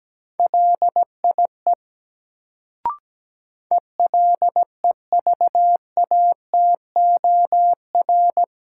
Morse code, heard as a keyed tone, about 17 wpm, 700 Hz. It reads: LIE ELEVATOR
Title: Sentences from Top 1600 Words - New Word Drill - 17wpm